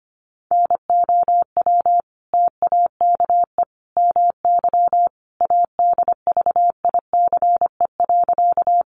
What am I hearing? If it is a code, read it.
Morse code, heard as a keyed tone, about 25 wpm, 700 Hz. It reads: NOW TAKE MY ADVICE.
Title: Sentences from Top 1100 Words 25wpm